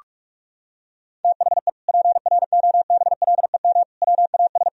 Morse code, heard as a keyed tone, about 47 wpm, 700 Hz. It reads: THE PROBLEM WAS
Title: Sentences from Top 1000 Words - Review Part 2 - 45wpm